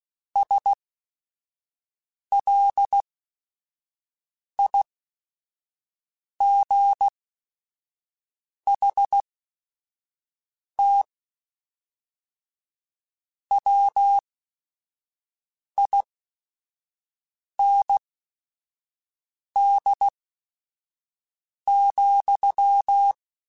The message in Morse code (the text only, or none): SLIGHT WIND,